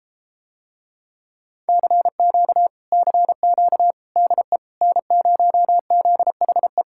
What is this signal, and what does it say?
Morse code, heard as a keyed tone, about 33 wpm, 700 Hz. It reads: CQ CQ DE N0ZHE